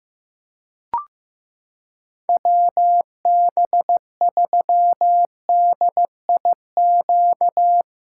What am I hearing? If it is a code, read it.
Morse code, heard as a keyed tone, about 16 wpm, 700 Hz. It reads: WB3DIQ
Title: USA Callsign DE Callsign - 6 Characters - 15wpm